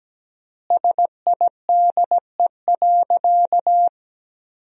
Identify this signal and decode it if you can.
Morse code, heard as a keyed tone, about 17 wpm, 700 Hz. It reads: SIDE.
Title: 3x Word-Spacing - Sentences from Top 400 Words - 17wpm